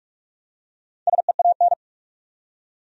Morse code, heard as a keyed tone, about 47 wpm, 700 Hz. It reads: SEAN
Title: USA Common Names - Max 5 Letters - 45wpm